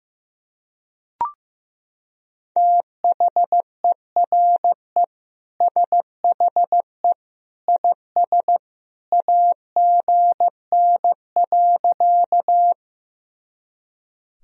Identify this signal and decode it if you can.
Morse code, heard as a keyed tone, about 15 wpm, 700 Hz. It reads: THERE SHE IS AGN.
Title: Sentences from Top 200 Words and Abbreviations 15wpm